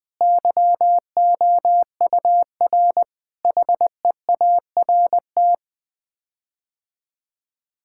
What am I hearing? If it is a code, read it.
Morse code, heard as a keyed tone, about 20 wpm, 700 Hz. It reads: YOUR HEART
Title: Sets of 9 Words - 20wpm